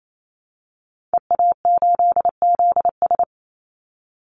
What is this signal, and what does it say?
Morse code, heard as a keyed tone, about 28 wpm, 700 Hz. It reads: EA8ZS
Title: Call Signs Repeated 3x - 28wpm